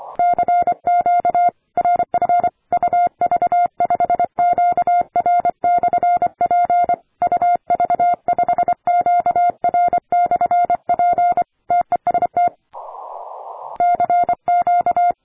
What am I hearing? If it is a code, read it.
Morse code, heard as a keyed tone, about 25 wpm, 700 Hz. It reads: CQ RF UV5QR/P UV5QR/P TEST CQ
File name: attachments / 40m UU7JF P